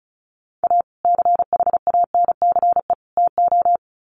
Morse code, heard as a keyed tone, about 34 wpm, 700 Hz. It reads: A CHANCE TO
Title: Sets of 3 Words 35wpm